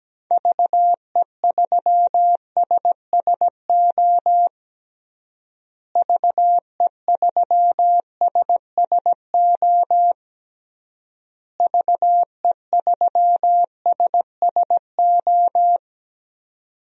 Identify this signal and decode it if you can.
Morse code, heard as a keyed tone, about 17 wpm, 700 Hz. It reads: VE3SSO VE3SSO VE3SSO
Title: DXCC-100 Call Signs - 6 Characters - Repeated 3x - 17wpm